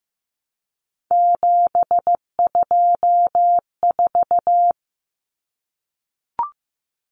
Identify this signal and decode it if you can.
Morse code, heard as a keyed tone, about 15 wpm, 700 Hz. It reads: 724
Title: Numbers - 3-Digit 15wpm